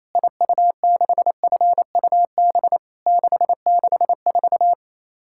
Morse code, heard as a keyed tone, about 28 wpm, 700 Hz. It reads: IU6FUB 664
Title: Worked All Europe DX Contest - QTC Exchange - 28wpm